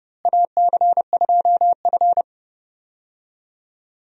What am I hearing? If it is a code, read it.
Morse code, heard as a keyed tone, about 29 wpm, 700 Hz. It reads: AC2F
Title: USA Call Signs - 4 Characters - Repeated 3x - 30wpm